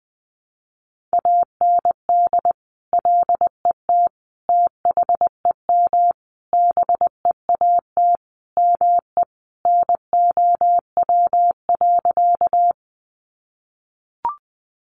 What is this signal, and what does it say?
Morse code, heard as a keyed tone, about 20 wpm, 700 Hz. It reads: AND LET THEM BEAT ME NOW.
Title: Sentences from Top 1300 Words 20wpm